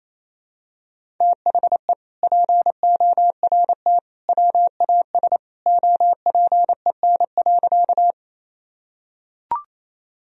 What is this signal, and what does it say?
Morse code, heard as a keyed tone, about 28 wpm, 700 Hz. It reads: THE PORT WAS OPEN.
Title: Sentences from Top 1000 Words - Review Part 1 - 28wpm